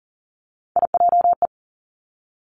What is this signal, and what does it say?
Morse code, heard as a keyed tone, about 41 wpm, 700 Hz. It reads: IJE
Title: Three Challenging Letter-Number Combinations 40wpm